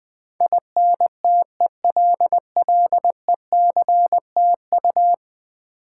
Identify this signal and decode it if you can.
Morse code, heard as a keyed tone, about 20 wpm, 700 Hz. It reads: INTELLECTU
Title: Build A Word - 5 Letters At A Time - 12-Letter Words - 20wpm